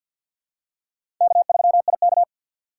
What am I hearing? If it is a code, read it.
Morse code, heard as a keyed tone, about 53 wpm, 700 Hz. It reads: K3IK
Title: USA Call Signs - 4 Characters - 3x Character Spacing - 50wpm